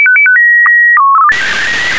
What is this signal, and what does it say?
SSTV (slow-scan television), PD 120.
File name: ariss-sstv / ariss- -1445